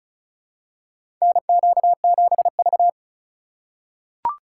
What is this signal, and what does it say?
Morse code, heard as a keyed tone, about 34 wpm, 700 Hz. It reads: NQ7V
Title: USA Call Signs - 4 Characters - 6x Character Spacing - 35wpm